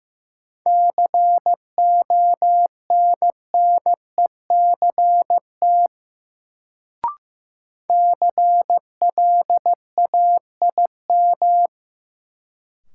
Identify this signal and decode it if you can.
Morse code, heard as a keyed tone, about 15 wpm, 700 Hz. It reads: CONNECT CLAIM
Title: Sentences from Top 1500 Words - New Word Drill - 15wpm